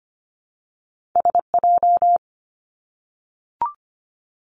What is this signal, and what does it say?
Morse code, heard as a keyed tone, about 25 wpm, 700 Hz. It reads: SJ
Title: Two Challenging Letter-Number Combinations 25wpm